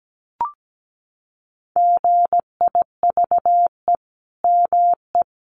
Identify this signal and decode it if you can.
Morse code, heard as a keyed tone, about 17 wpm, 700 Hz. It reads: GIVE ME
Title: Sentences from Top 1400 Words 17wpm